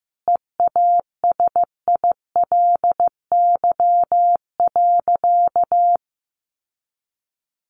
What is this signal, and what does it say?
Morse code, heard as a keyed tone, about 15 wpm, 700 Hz. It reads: EASILY.
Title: Sentences from Top 900 Words 15wpm